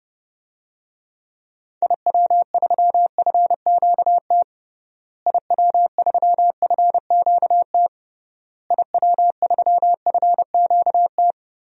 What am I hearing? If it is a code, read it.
Morse code, heard as a keyed tone, about 32 wpm, 700 Hz. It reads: IW3FQT IW3FQT IW3FQT
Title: DXCC-100 Call Signs - 6 Characters - Repeated 3x - 30wpm